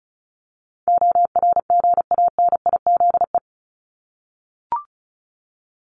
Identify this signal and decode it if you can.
Morse code, heard as a keyed tone, about 34 wpm, 700 Hz. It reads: ORGANIZE
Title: Top 1751-2000 Words 35wpm